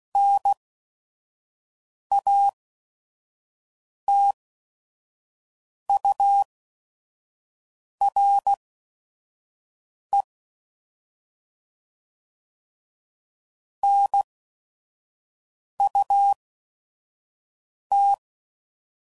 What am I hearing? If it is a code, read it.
Morse code, heard as a keyed tone, about 16 wpm, spaced as for 5 wpm, 800 Hz. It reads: NATURE NUT
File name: Prefixes / By First Letter / Ns